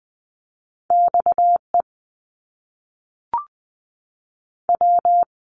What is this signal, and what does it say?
Morse code, heard as a keyed tone, about 20 wpm, 700 Hz. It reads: XE W
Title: Top Two Letter Combinations in Words 20wpm